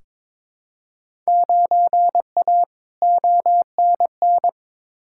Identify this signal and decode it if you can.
Morse code, heard as a keyed tone, about 22 wpm, 700 Hz. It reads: 9A ONN
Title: Field Day Exchanges Encore - 22wpm